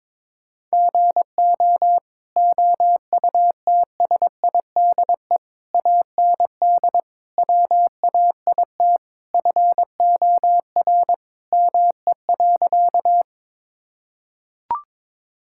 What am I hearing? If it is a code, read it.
Morse code, heard as a keyed tone, about 22 wpm, 700 Hz. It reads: GO OUTSIDE AND WAIT FOR ME.
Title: Sentences from Top 1000 Words - Review Part 6 - 22wpm